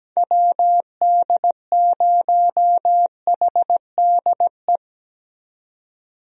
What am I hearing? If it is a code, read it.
Morse code, heard as a keyed tone, about 17 wpm, 700 Hz. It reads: WD0HDE
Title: USA Call Signs - 6 Characters - 17wpm